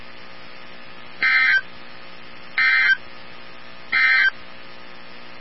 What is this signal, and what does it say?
Emergency Alert System (SAME) tones, heard as harsh buzzing bursts. It is heard from 1.1 to 4.4 s.
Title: NOAA Weather Radio 162.550